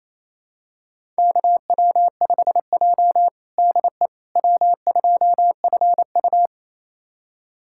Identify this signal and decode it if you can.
Morse code, heard as a keyed tone, about 28 wpm, 700 Hz. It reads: KW5J DE W2FU
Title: USA Callsign DE Callsign - 4 Characters - 28wpm